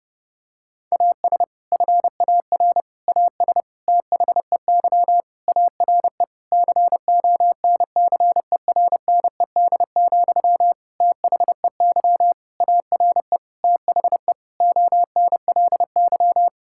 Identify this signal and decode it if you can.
Morse code, heard as a keyed tone, about 30 wpm, 700 Hz. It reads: AS FAR AS THEY ARE CONCERNED, THEY ARE THE ONLY
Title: Sentences from Top 1000 Words - Review Part 6 - 30wpm